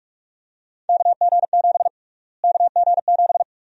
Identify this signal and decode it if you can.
Morse code, heard as a keyed tone, about 46 wpm, 700 Hz. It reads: KG7 KG7
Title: USA Call Sign Prefixes - Repeated 2x - 45wpm